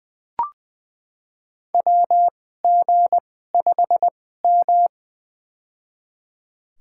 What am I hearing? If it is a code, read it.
Morse code, heard as a keyed tone, about 21 wpm, spaced as for 15 wpm, 700 Hz. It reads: WG5M
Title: USA Call Signs - 4 Characters - 2x Character Spacing - 20wpm